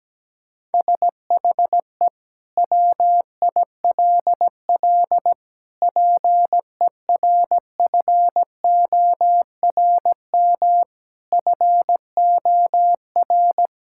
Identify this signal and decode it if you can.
Morse code, heard as a keyed tone, about 17 wpm, 700 Hz. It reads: SHE WILL PERFORM FOR
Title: Sentences from Top 1000 Words - Review Part 5 - 17wpm